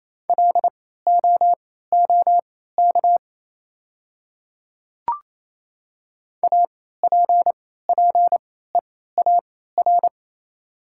Morse code, heard as a keyed tone, about 28 wpm, spaced as for 16 wpm, 690 Hz. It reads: LOOK APPEAR
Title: 3x Character Spacing - Top 500 Words - 28wpm